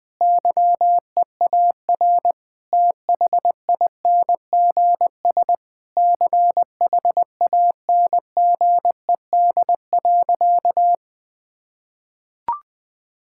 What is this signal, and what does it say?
Morse code, heard as a keyed tone, about 21 wpm, 700 Hz. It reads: YEAR THINGS CHANGED.
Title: Sentences with 5-Word Phrases - Spoken Beginning - 20wpm